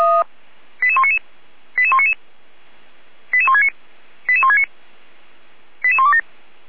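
Five-tone selective call (ZVEI), heard as quick runs of five tones, tones of about 70 ms. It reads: 88199 88199 88169 88169 88127
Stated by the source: ZVEI five-tone selective call received by SDR.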